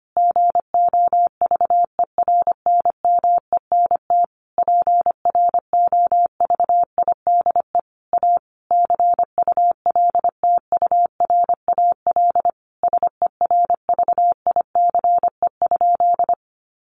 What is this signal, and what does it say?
Morse code, heard as a keyed tone, about 25 wpm, 700 Hz. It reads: GOVERNMENT PROVIDE A CULTURAL SERVICE?
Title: Sentences from Top 1500 Words 25wpm